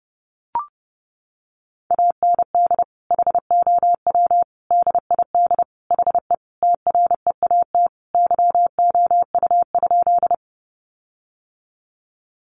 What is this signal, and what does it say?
Morse code, heard as a keyed tone, about 30 wpm, 700 Hz. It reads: AND HOW DID HE TREAT YOU?
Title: Sentences from Top 1300 Words 30wpm